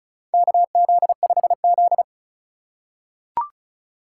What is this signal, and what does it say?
Morse code, heard as a keyed tone, about 36 wpm, 700 Hz. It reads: KZ5Z
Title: K1USN SST QSO Practice - 35wpm